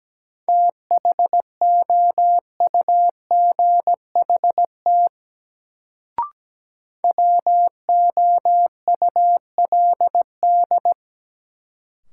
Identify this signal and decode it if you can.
Morse code, heard as a keyed tone, about 17 wpm, 700 Hz. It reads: THOUGHT WOULD
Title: Sets of 2 Words - New Word Drill - 17wpm